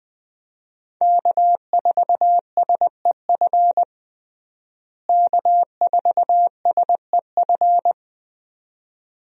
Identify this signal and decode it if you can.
Morse code, heard as a keyed tone, about 20 wpm, 700 Hz. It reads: K4SEF K4SEF
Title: USA Call Signs - 5 Characters - Repeated 2x - 20wpm